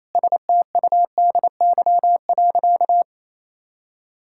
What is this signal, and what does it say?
Morse code, heard as a keyed tone, about 28 wpm, 700 Hz. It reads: STUDY.
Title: Sentences from Top 1000 Words - Review Part 5 - 28wpm